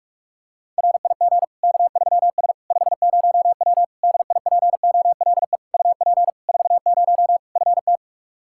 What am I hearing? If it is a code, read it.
Morse code, heard as a keyed tone, about 46 wpm, 700 Hz. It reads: RIG K3S 50W DIPOLE UP 40 FT
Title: K1USN SST QSO Practice 45wpm